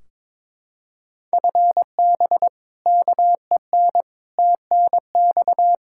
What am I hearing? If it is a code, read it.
Morse code, heard as a keyed tone, about 22 wpm, 700 Hz. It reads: FB KEN TNX
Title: Practice QSO - Exchange 3 - More Info - 22wpm